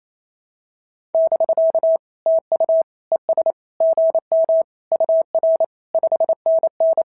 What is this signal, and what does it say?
Morse code, heard as a keyed tone, about 25 wpm, 650 Hz. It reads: <BK> TU ES GM UR 5NN
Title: SOTA QSO Practice 28wpm